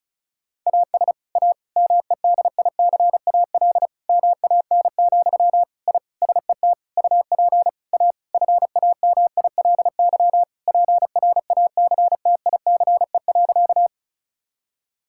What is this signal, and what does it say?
Morse code, heard as a keyed tone, about 36 wpm, 700 Hz. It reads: AS A MEDICAL MAN, I SET UP A FAMILY PRACTICE.
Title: Sentences from Top 1000 Words - Review Part 5 - 35wpm